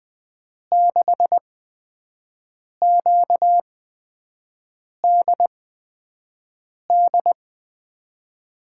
Morse code, heard as a keyed tone, about 20 wpm, spaced as for 5 wpm, 700 Hz. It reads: 6QDD
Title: USA Call Signs - 6 Characters - 8x Character Spacing - 20wpm